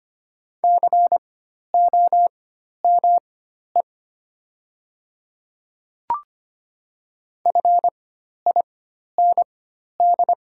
Morse code, heard as a keyed tone, about 25 wpm, spaced as for 12 wpm, 700 Hz. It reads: COME FIND
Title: 4x Character Spacing - Top 200 Words - 25wpm